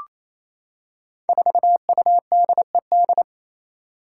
Morse code, heard as a keyed tone, about 27 wpm, 700 Hz. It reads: 4UDED
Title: Five Challenging Letter-Number Combinations 28wpm